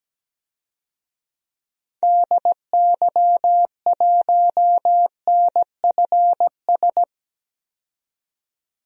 Morse code, heard as a keyed tone, about 17 wpm, 700 Hz. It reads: DY1NFS
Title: DXCC-100 Call Signs - 6 Characters - 17wpm